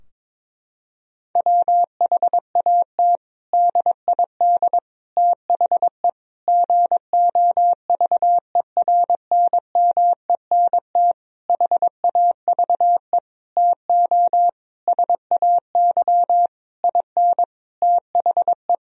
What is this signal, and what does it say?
Morse code, heard as a keyed tone, about 22 wpm, 700 Hz. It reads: WHAT DID THE GOVERNMENT HAVE TO SAY IN THE
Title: Sentences from Top 1000 Words - Review Part 3 - 22wpm